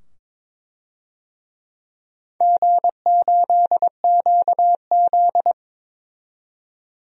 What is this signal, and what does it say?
Morse code, heard as a keyed tone, about 22 wpm, 700 Hz. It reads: G8QZ
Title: DXCC-100 Call Signs - 4 Characters - 22wpm